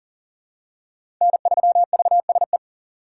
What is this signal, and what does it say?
Morse code, heard as a keyed tone, about 37 wpm, 700 Hz. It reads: N3VSE
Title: USA Call Signs - 5 Characters - 6x Character Spacing - 40wpm